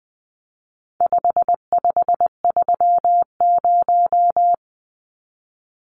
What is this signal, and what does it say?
Morse code, heard as a keyed tone, about 20 wpm, 700 Hz. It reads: 5530
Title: Numbers - 4-Digit 20wpm